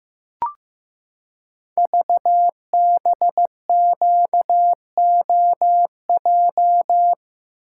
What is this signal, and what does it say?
Morse code, heard as a keyed tone, about 15 wpm, 700 Hz. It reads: VBQOJ
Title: Code Groups - 5 Letters - 15wpm